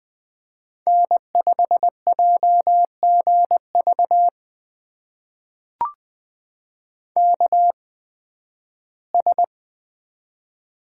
Morse code, heard as a keyed tone, about 20 wpm, spaced as for 13 wpm, 700 Hz. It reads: N5JGV KS
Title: USA Call Signs - 8x Character Spacing - 20wpm